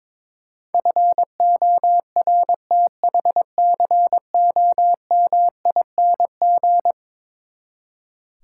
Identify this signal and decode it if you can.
Morse code, heard as a keyed tone, about 22 wpm, 700 Hz. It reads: FORTHCOMING
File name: practice / 22wpm / Eleven Letter Words - 22wpm